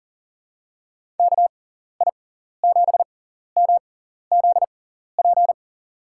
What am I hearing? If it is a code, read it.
Morse code, heard as a keyed tone, about 39 wpm, spaced as for 14 wpm, 700 Hz. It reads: KI7MZP
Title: USA Call Signs - 6 Characters - 6x Character Spacing - 40wpm